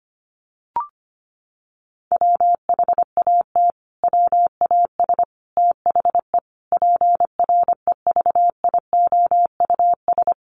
Morse code, heard as a keyed tone, about 25 wpm, 700 Hz. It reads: WHAT WAS THE PREVIOUS
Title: Sentences from Top 1500 Words 25wpm